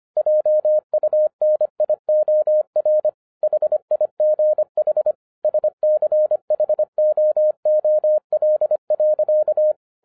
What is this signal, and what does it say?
Morse code, heard as a keyed tone, about 25 wpm, 610 Hz. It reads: JUNIOR HIGH SCHOOL.